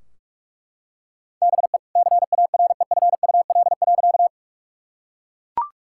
Morse code, heard as a keyed tone, about 42 wpm, 700 Hz. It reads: BE CAREFUL.
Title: Sentences from Top 1600 Words 45wpm